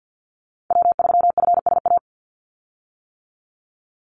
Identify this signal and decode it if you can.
Morse code, heard as a keyed tone, about 50 wpm, 700 Hz. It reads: W3FSA
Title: Call Signs - 50wpm